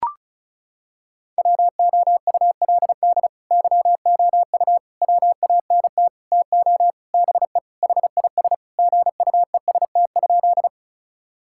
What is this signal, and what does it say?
Morse code, heard as a keyed tone, about 34 wpm, 700 Hz. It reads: WOULD YOU WANT TO BE HIS GUEST?
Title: Sentences from Top 1600 Words 35wpm